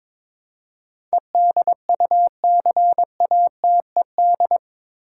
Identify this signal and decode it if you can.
Morse code, heard as a keyed tone, about 22 wpm, 700 Hz. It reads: EDUCATED
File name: practice / 22wpm / Eight Letter Words - 22wpm